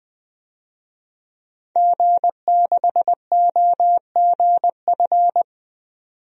Morse code, heard as a keyed tone, about 20 wpm, 700 Hz. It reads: G6OGF
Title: DXCC-100 Call Signs - 5 Characters - 20wpm